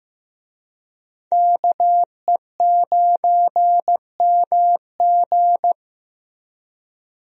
Morse code, heard as a keyed tone, about 15 wpm, 700 Hz. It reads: KE9MG
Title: USA Call Signs - 5 Characters - Repeated 3x - 15wpm